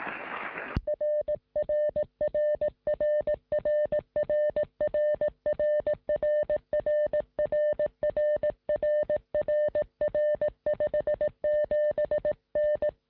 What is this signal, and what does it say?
Morse code, heard as a keyed tone, about 18 wpm, 600 Hz. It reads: RRRRRRRRRRRRRRR57N
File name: sm7lcb / 1958z ok1ri 2m